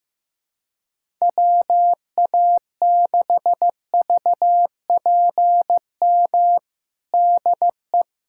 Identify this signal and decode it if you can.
Morse code, heard as a keyed tone, about 15 wpm, 700 Hz. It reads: WA6VPM DE
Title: USA Callsign DE Callsign - 6 Characters - 15wpm